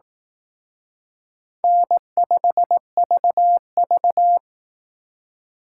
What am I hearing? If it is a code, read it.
Morse code, heard as a keyed tone, about 18 wpm, 700 Hz. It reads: N5VV
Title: Speed Racing - USA 1x2 Call Signs 15wpm